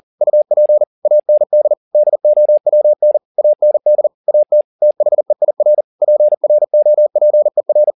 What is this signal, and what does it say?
Morse code, heard as a keyed tone, about 40 wpm, 600 Hz. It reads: UP AND DOWN AND AT THEIR PROPER